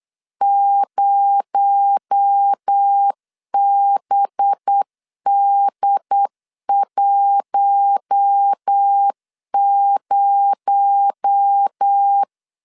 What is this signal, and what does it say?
Morse code, heard as a keyed tone, about 8 wpm, 800 Hz. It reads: 0BD10
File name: LEZIONI / 6 PKX0 / 6 PKX0 8wpm 3341